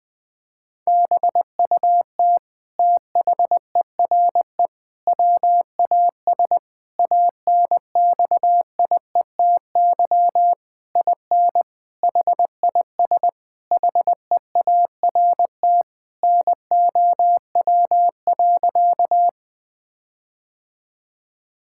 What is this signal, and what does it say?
Morse code, heard as a keyed tone, about 20 wpm, 700 Hz. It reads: BUT THERE WAS ANXIETY IN HIS HEART NOW.
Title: Sentences from Top 1600 Words 20wpm